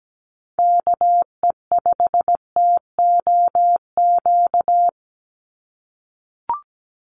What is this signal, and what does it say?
Morse code, heard as a keyed tone, about 17 wpm, 700 Hz. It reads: KE5TOQ
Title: USA Call Signs 17wpm